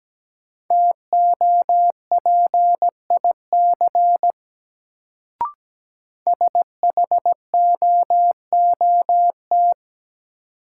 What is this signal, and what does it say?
Morse code, heard as a keyed tone, about 17 wpm, 700 Hz. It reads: TOPIC SHOOT
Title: Sentences from Top 1200 Words - New Word Drill Alt - 17wpm